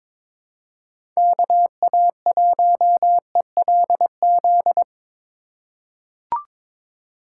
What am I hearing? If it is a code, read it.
Morse code, heard as a keyed tone, about 22 wpm, 700 Hz. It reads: KA1ELZ
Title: USA Call Signs - 6 Characters - 4x Character Spacing - 22wpm